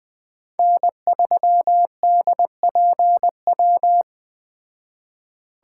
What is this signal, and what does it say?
Morse code, heard as a keyed tone, about 20 wpm, 700 Hz. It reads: N3DPW
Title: Speed Racing - USA 1x3 Call Signs 17wpm